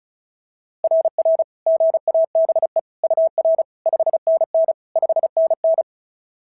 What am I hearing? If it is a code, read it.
Morse code, heard as a keyed tone, about 36 wpm, 690 Hz. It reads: RR GABE UR 5NN 5NN
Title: SOTA QSO Practice 35wpm